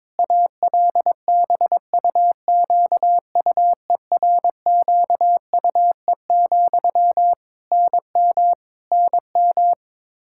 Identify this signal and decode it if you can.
Morse code, heard as a keyed tone, about 22 wpm, 700 Hz. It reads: ALBUQUERQUE, NM NM
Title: Practice QSO - Exchange 1 - Beginning - 22wpm